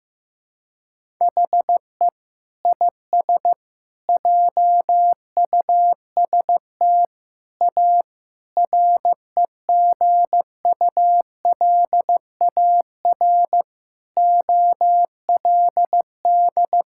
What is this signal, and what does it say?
Morse code, heard as a keyed tone, about 15 wpm, 700 Hz. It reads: HE IS JUST A REGULAR OLD
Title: Sentences from Top 1000 Words - Review Part 6 - 15wpm